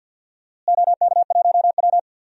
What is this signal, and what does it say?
Morse code, heard as a keyed tone, about 51 wpm, 700 Hz. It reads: OK1W
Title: DXCC-100 Call Signs - 4 Characters - Repeated 2x - 50wpm